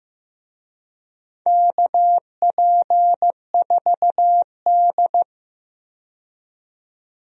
Morse code, heard as a keyed tone, about 15 wpm, 700 Hz. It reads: KP4D
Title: DXCC-100 Call Signs - 4 Characters - 15wpm